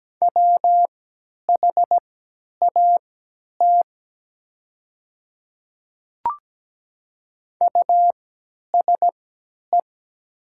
Morse code, heard as a keyed tone, about 17 wpm, spaced as for 10 wpm, 700 Hz. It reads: WHAT USE
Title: 3x Character Spacing - Top 100 Words - 17wpm